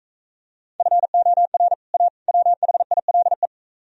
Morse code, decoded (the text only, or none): FOR A WHILE